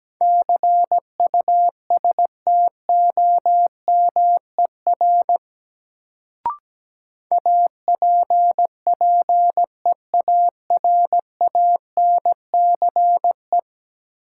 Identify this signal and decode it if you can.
Morse code, heard as a keyed tone, about 17 wpm, 700 Hz. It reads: CUSTOMER APPEARANCE
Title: Sentences from Top 1100 Words - New Word Drill - 17wpm